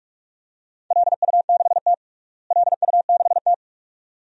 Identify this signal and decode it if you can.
Morse code, heard as a keyed tone, about 47 wpm, 700 Hz. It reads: LU6T LU6T
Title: DXCC-100 Call Signs - 4 Characters - Repeated 2x - 45wpm